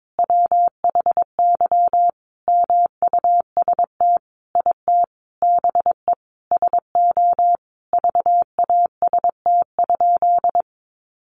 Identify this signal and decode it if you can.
Morse code, heard as a keyed tone, about 22 wpm, 700 Hz. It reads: WHY MUST IT BE SO VAST?
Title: Sentences from Top 1300 Words 22wpm